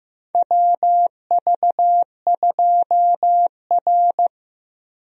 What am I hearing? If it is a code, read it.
Morse code, heard as a keyed tone, about 15 wpm, 700 Hz. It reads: WV2R
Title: USA Call Signs - 4 Characters - Repeated 2x - 15wpm